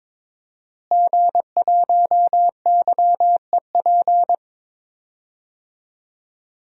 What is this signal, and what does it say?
Morse code, heard as a keyed tone, about 21 wpm, 700 Hz. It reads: G1YEP
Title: DXCC-100 Call Signs - 5 Characters - Repeated 2x - 22wpm